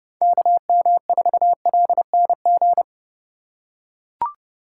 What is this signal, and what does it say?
Morse code, heard as a keyed tone, about 30 wpm, 700 Hz. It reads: KM4LNG
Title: USA Call Signs - 6x Character Spacing - 30wpm